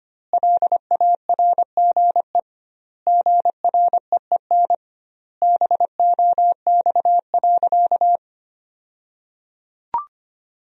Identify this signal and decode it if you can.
Morse code, heard as a keyed tone, about 25 wpm, 700 Hz. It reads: LARGE GREEN BOX.
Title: 2x Word-Spacing - Sentences from Top 500 Words - 25wpm